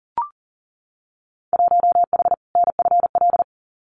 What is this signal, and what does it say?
Morse code, heard as a keyed tone, about 38 wpm, 700 Hz. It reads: 1H NFL
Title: Winter Field Day Exchanges 40wpm